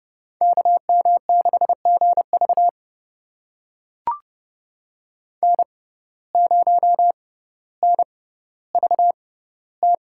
Morse code, heard as a keyed tone, about 30 wpm, spaced as for 15 wpm, 700 Hz. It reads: KM6GV N0NVT
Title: USA Call Signs - 5 Characters - 6x Character Spacing - 30wpm